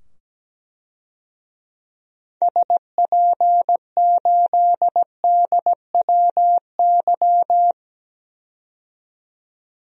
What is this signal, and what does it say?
Morse code, heard as a keyed tone, about 17 wpm, 700 Hz. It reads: SP8DWY
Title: DXCC-100 Call Signs - 6 Characters - 17wpm